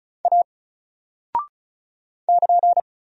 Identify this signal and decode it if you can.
Morse code, heard as a keyed tone, about 36 wpm, 700 Hz. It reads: A <KN>
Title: Single Character (Letter:Number:Punctuation:Prosign) 35wpm